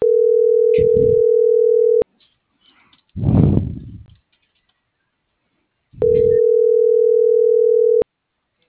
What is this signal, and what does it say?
Ringback tone (North America), 440 + 480 Hz.